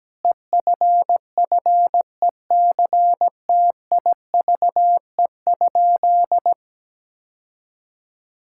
Morse code, heard as a keyed tone, about 17 wpm, 700 Hz. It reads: EFFECTIVE?
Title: Sentences from Top 1000 Words - Review Part 6 - 17wpm